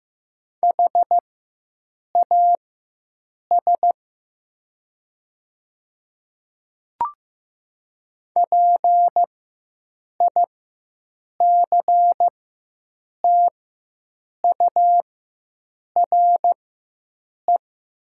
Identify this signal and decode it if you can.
Morse code, heard as a keyed tone, about 15 wpm, spaced as for 7 wpm, 700 Hz. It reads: HAS PICTURE
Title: 4x Character Spacing - Top 200 Words - 15wpm